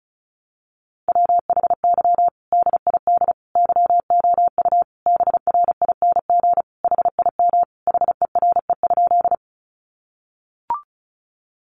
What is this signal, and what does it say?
Morse code, heard as a keyed tone, about 34 wpm, 700 Hz. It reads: WHY DID YOU BRING HIM HERE?
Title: Sentences from Top 500 Words 35wpm